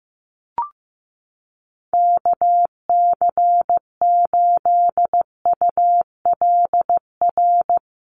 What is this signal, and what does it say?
Morse code, heard as a keyed tone, about 15 wpm, 700 Hz. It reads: KC8ULR
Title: USA Call Signs Repeated 2x 15wpm